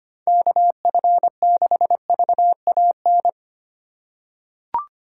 Morse code, heard as a keyed tone, about 25 wpm, 700 Hz. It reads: KF6VAN
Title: USA Call Signs - 6 Characters - 4x Character Spacing - 25wpm